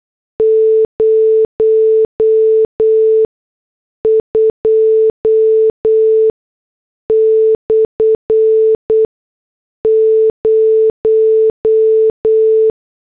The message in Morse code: 02/0